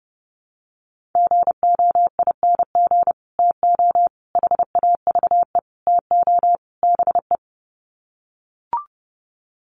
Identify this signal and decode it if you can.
Morse code, heard as a keyed tone, about 30 wpm, 700 Hz. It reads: GOING TO HAVE TO BE
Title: Sets of 5 Words 30wpm